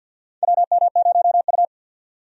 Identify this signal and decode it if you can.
Morse code, heard as a keyed tone, about 51 wpm, 700 Hz. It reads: WM0U
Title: USA Call Signs - 4 Characters - 50wpm